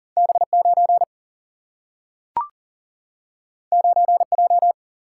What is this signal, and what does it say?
Morse code, heard as a keyed tone, about 40 wpm, 700 Hz. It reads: B9 9J
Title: Single-Letter Single-Number 40wpm